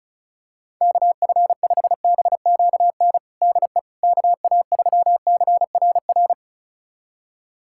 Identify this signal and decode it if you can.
Morse code, heard as a keyed tone, about 36 wpm, 700 Hz. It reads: KF5BQN DE KA3CRR
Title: USA Callsign DE Callsign - 6 Characters - 35wpm